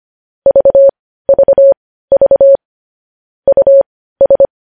Morse code, heard as a keyed tone, about 25 wpm, spaced as for 15 wpm, 550 Hz. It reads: VVV US
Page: news-headlines-in-morse-code-at-15-wpm